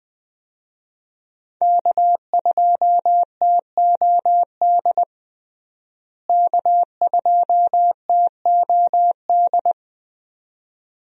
Morse code, heard as a keyed tone, about 20 wpm, 700 Hz. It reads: K2TOD K2TOD
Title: USA Call Signs - 5 Characters - Repeated 2x - 20wpm